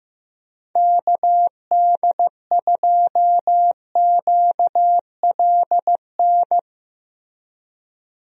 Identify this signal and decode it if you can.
Morse code, heard as a keyed tone, about 15 wpm, 700 Hz. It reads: KD2QLN
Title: USA Call Signs - 6 Characters - Repeated 2x - 15wpm